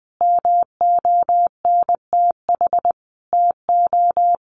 Morse code, heard as a keyed tone, about 19 wpm, 700 Hz. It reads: MONTH TO
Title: Sentences from Top 1200 Words 20wpm